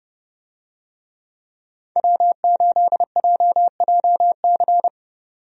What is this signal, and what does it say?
Morse code, heard as a keyed tone, about 30 wpm, 700 Hz. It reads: W8JJC
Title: USA Call Signs - 5 Characters - 30wpm